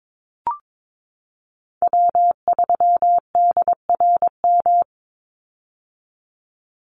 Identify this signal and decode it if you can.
Morse code, heard as a keyed tone, about 22 wpm, 700 Hz. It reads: W3DRM
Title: USA Call Signs 22wpm